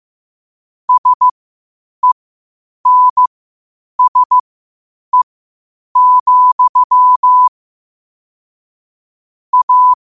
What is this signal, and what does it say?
Morse code, heard as a keyed tone, about 15 wpm, spaced as for 8 wpm, 1000 Hz. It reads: SENSE, A